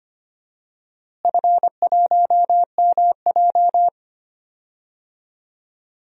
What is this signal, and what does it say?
Morse code, heard as a keyed tone, about 25 wpm, 700 Hz. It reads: F1MJ
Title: DXCC-100 Call Signs - 4 Characters - Repeated 3x - 25wpm